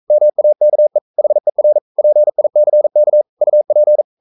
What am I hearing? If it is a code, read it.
Morse code, heard as a keyed tone, about 42 wpm, 600 Hz. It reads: MAKE HER PICK UP
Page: Romanen Never af Ken Follet på engelsk.